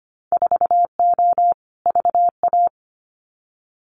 Morse code, heard as a keyed tone, about 25 wpm, 700 Hz. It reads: 4O VA
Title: Winter Field Day Exchanges Encore 25wpm